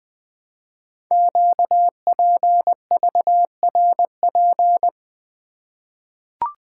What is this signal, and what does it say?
Morse code, heard as a keyed tone, about 20 wpm, 700 Hz. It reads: QPVRP
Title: Code Groups - 5 Letters - 20wpm